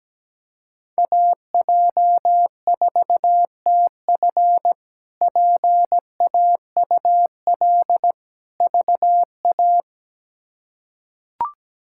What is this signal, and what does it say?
Morse code, heard as a keyed tone, about 17 wpm, 700 Hz. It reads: AJ4TF PAUL VA
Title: K1USN SST QSO Practice - 17wpm